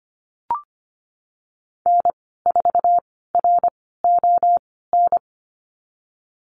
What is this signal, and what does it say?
Morse code, heard as a keyed tone, about 25 wpm, spaced as for 16 wpm, 700 Hz. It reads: N4RON
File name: Lesson 15g - Number 5 - Call Signs 25wpm